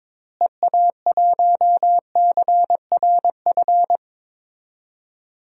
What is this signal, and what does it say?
Morse code, heard as a keyed tone, about 22 wpm, 700 Hz. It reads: EA1CRF
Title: DXCC-100 Call Signs - 6 Characters - Repeated 2x - 22wpm